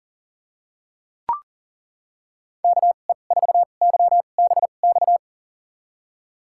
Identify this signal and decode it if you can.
Morse code, heard as a keyed tone, about 40 wpm, spaced as for 29 wpm, 700 Hz. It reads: KE4YBX
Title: USA Call Signs - 6 Characters - 2x Character Spacing - 40wpm